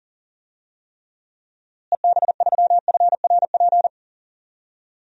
Morse code, heard as a keyed tone, about 40 wpm, 700 Hz. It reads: EB3FRP
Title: DXCC-100 Call Signs - 6 Characters - Repeated 2x - 40wpm